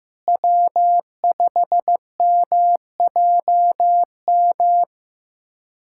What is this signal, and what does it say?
Morse code, heard as a keyed tone, about 15 wpm, 700 Hz. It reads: W5MJM
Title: USA Callsign DE Callsign - 5 Characters - 15wpm